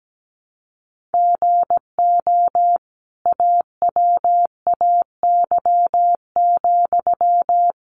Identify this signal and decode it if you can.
Morse code, heard as a keyed tone, about 17 wpm, 700 Hz. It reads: GO AWAY,
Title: Sentences from Top 1500 Words 17wpm